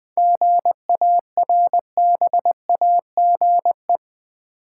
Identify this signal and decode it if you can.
Morse code, heard as a keyed tone, about 20 wpm, 700 Hz. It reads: GARBAGE
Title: Seven Letter Words - 20wpm